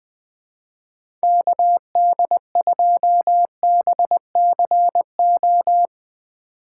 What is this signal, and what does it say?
Morse code, heard as a keyed tone, about 20 wpm, 700 Hz. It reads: KD2BCO
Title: Speed Racing - USA 2x3 Call Signs 17wpm